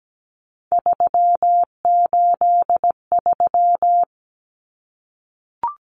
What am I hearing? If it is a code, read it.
Morse code, heard as a keyed tone, about 17 wpm, 700 Hz. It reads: 383
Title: Numbers - 3-Digit 17wpm